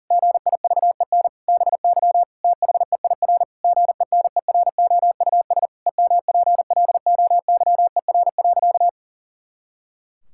Morse code, heard as a keyed tone, about 40 wpm, 700 Hz. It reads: GIVEN BY THEIR GENEROUS EMPLOYER.
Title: Pangram Challenge v2 - 40wpm